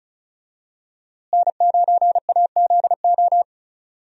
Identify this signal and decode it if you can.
Morse code, heard as a keyed tone, about 34 wpm, 700 Hz. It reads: N9AZO
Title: USA Call Signs - 5 Characters - Repeated 2x - 35wpm